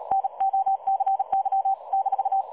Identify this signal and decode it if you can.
Morse code, heard as a keyed tone, about 39 wpm, 780 Hz. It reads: NOCY -
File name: dlazaawansowanych / nagraniazpasma / websdr recording 3565.0kHz